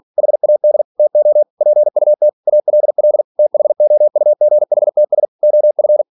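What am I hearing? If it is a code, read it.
Morse code, heard as a keyed tone, about 46 wpm, 600 Hz. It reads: HAD TO PUT ALL THOUGHTS OF
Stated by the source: Practice text from an English novel sent at 47 WPM.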